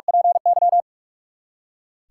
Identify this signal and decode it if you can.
Morse code, heard as a keyed tone, about 46 wpm, 700 Hz. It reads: PY
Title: Practice QSO - Exchange 3 - More Info - 45wpm